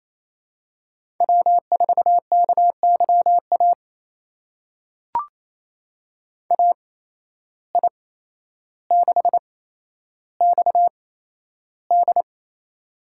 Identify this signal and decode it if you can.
Morse code, heard as a keyed tone, about 28 wpm, spaced as for 12 wpm, 700 Hz. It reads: W4KYA AI6XD
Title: USA Call Signs - 5 Characters - 8x Character Spacing - 28wpm